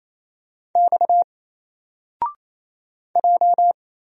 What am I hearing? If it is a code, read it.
Morse code, heard as a keyed tone, about 27 wpm, 700 Hz. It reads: X J
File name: Single-Letter-Number-28wpm